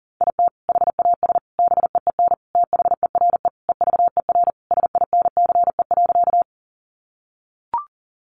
Morse code, heard as a keyed tone, about 40 wpm, 700 Hz. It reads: IT HAS BEEN THERE EVER SINCE.
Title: Sentences from Top 400 Words 40wpm